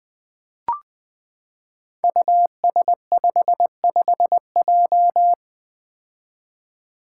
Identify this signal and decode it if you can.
Morse code, heard as a keyed tone, about 20 wpm, 700 Hz. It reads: US55J
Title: Five Challenging Letter-Number Combinations 20wpm